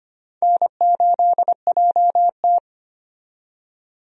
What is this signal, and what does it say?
Morse code, heard as a keyed tone, about 25 wpm, 700 Hz. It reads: N8JT
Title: USA Call Signs - 4 Characters - 25wpm